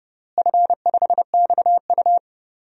Morse code, heard as a keyed tone, about 30 wpm, 700 Hz. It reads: F5XU
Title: DXCC-100 Call Signs - 4 Characters - Repeated 2x - 30wpm